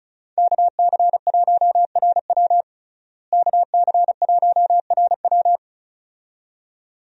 Morse code, heard as a keyed tone, about 36 wpm, 700 Hz. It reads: KC1RW KC1RW
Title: USA Call Signs - 5 Characters - Repeated 2x - 35wpm